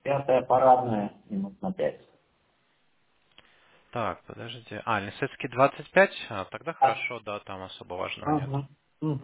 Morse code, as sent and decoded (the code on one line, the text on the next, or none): none